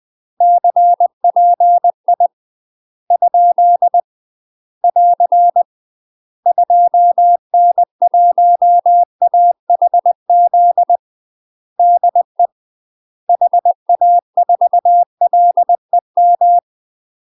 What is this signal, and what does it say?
Morse code, heard as a keyed tone, about 20 wpm, 700 Hz. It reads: CPI ? <AR> 2N1AHZ DE HA4LEM